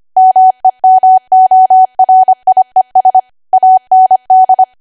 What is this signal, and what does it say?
Morse code, heard as a keyed tone, about 25 wpm, 750 Hz. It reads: MEMORIES AND
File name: Archive / 25 WPM / 25WPM